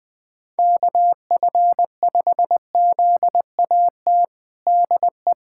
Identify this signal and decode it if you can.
Morse code, heard as a keyed tone, about 21 wpm, 700 Hz. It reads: KF5ZAT DE
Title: USA Callsign DE Callsign - 6 Characters - 20wpm